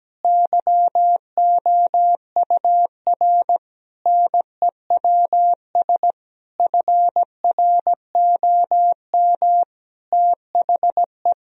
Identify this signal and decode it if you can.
Morse code, heard as a keyed tone, about 17 wpm, 700 Hz. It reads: YOUR NEWS FROM THE
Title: Sentences from Top 1000 Words - Review Part 4 - 17wpm